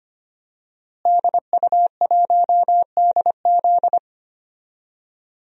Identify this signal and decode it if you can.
Morse code, heard as a keyed tone, about 25 wpm, 700 Hz. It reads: DU1DZ
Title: DXCC-100 Call Signs - 5 Characters - Repeated 3x - 25wpm